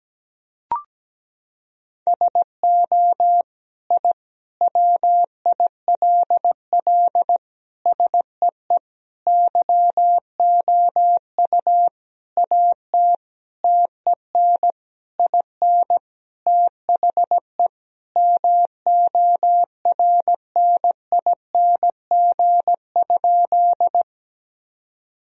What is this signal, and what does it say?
Morse code, heard as a keyed tone, about 17 wpm, 700 Hz. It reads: SO I WILL SEE YOU AT TEN IN THE MORNING?
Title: Sentences from Top 1000 Words - Review Part 2 - 17wpm